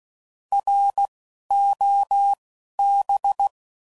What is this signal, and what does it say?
Morse code, heard as a keyed tone, about 16 wpm, 800 Hz. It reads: R O B
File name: Prefixes / By First Letter / Rs